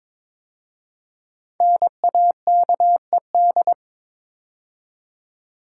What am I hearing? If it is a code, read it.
Morse code, heard as a keyed tone, about 22 wpm, 700 Hz. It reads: NAKED
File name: 5 Letter Words - First 3 Letters Given - 22wpm